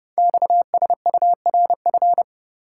Morse code, heard as a keyed tone, about 30 wpm, 700 Hz. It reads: XSURF
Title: Code Groups - 5 Letters - 30wpm